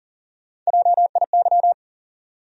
Morse code, heard as a keyed tone, about 40 wpm, 700 Hz. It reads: JIY
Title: USA Call Sign Postfixes - 3 Letters 40wpm